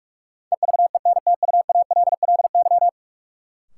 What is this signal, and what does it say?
Morse code, heard as a keyed tone, about 46 wpm, 700 Hz. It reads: EVENTUALLY
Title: Sentences from Top 1000 Words - Review Part 6 - New Word Drill - 45wpm